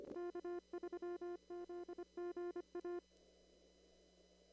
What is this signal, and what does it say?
Morse code, heard as a keyed tone, about 25 wpm, 1400 Hz. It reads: K3ZGA
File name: CW / kt5j wpx cw-op-k5pi